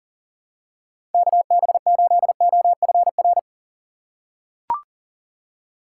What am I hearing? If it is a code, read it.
Morse code, heard as a keyed tone, about 37 wpm, 700 Hz. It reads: KB8OFR
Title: USA Call Signs - 6 Characters - 4x Character Spacing - 40wpm